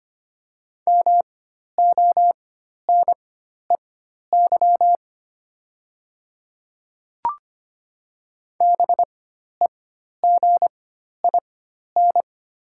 Morse code, heard as a keyed tone, about 25 wpm, spaced as for 12 wpm, 700 Hz. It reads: MONEY BEGIN
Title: 4x Character Spacing - Top 500 Words - 25wpm